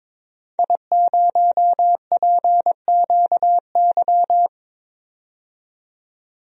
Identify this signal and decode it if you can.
Morse code, heard as a keyed tone, about 22 wpm, 700 Hz. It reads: I0PQY
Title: DXCC-100 Call Signs - 5 Characters - 22wpm